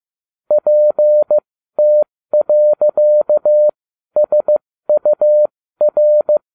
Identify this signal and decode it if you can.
Morse code, heard as a keyed tone, about 15 wpm, spaced as for 12 wpm, 600 Hz. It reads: PT.SUR